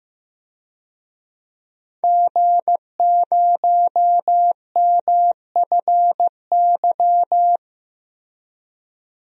Morse code, heard as a keyed tone, about 15 wpm, 700 Hz. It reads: G0MFY
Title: DXCC-100 Call Signs - 5 Characters - Repeated 3x - 15wpm